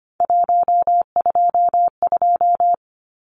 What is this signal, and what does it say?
Morse code, heard as a keyed tone, about 25 wpm, 700 Hz. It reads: 122
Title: Numbers - 3-Digit 25wpm